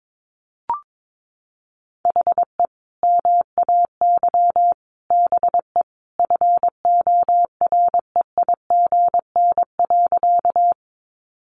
Morse code, heard as a keyed tone, about 22 wpm, 700 Hz. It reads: HE MAY BE FOREIGN.
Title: Sentences from Top 1400 Words 22wpm